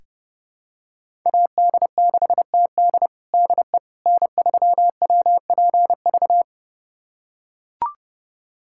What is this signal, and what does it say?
Morse code, heard as a keyed tone, about 31 wpm, 700 Hz. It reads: AD6TD DE N3WPV
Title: USA Callsign DE Callsign - 5 Characters - 30wpm